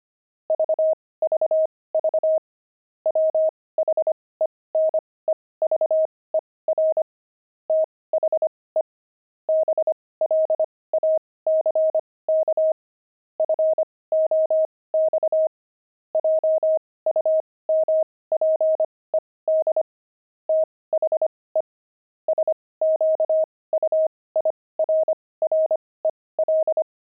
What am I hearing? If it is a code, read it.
Morse code, heard as a keyed tone, about 25 wpm, spaced as for 18 wpm, 630 Hz. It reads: VVV WHENEVER THE BLACK FOX JUMPED THE SQUIRREL
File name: pangram-fs-18-cs-25-squirrel